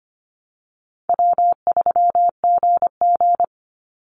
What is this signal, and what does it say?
Morse code, heard as a keyed tone, about 25 wpm, 700 Hz. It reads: W3GG
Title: Easy Call Signs Repeated 2x - 25wpm